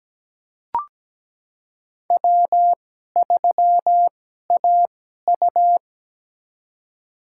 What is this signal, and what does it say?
Morse code, heard as a keyed tone, about 17 wpm, spaced as for 12 wpm, 700 Hz. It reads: W3AU
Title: USA Call Signs - 2x Character Spacing - 17wpm